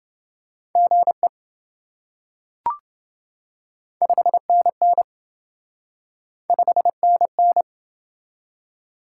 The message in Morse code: GE 5NN 5NN

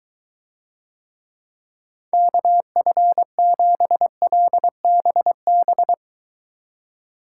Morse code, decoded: KF7LBB

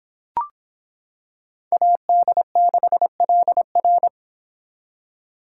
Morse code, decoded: AD6LR

